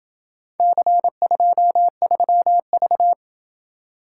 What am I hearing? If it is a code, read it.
Morse code, heard as a keyed tone, about 27 wpm, 700 Hz. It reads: C23V